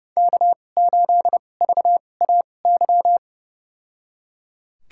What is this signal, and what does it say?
Morse code, heard as a keyed tone, about 30 wpm, spaced as for 22 wpm, 700 Hz. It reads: K8VAY
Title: USA Call Signs - 5 Characters - 2x Character Spacing - 30wpm